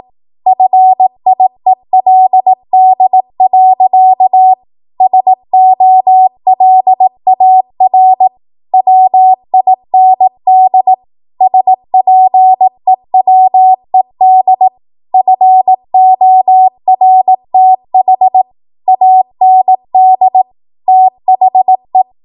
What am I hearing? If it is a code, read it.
Morse code, heard as a keyed tone, about 18 wpm, 750 Hz. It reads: FIELD. SOLAR WIND SPEWED FORTH AND THE
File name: Archive / 18 WPM / 18WPM